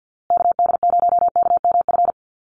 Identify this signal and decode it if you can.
Morse code, heard as a keyed tone, about 50 wpm, 700 Hz. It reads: KD0KMF
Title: USA Call Signs Repeated 2x 50wpm